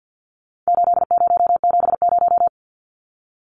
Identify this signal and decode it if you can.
Morse code, heard as a keyed tone, about 50 wpm, 700 Hz. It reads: 8070
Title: Numbers - 4-Digit 50wpm